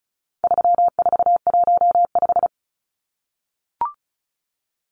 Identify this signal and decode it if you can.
Morse code, heard as a keyed tone, about 34 wpm, 700 Hz. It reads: 3415
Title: Numbers - 4-Digit 35wpm